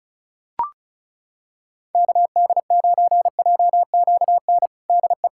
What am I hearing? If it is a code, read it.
Morse code, heard as a keyed tone, about 36 wpm, 700 Hz. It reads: KD9JQN DE
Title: Practice QSO - Exchange 4 - Ending - 35wpm